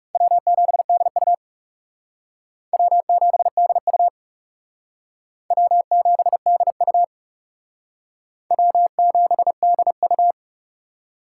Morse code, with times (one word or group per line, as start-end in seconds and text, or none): 0.1-1.3 W7DU
2.7-4.1 W7DU
5.5-7.0 W7DU
8.5-10.3 W7DU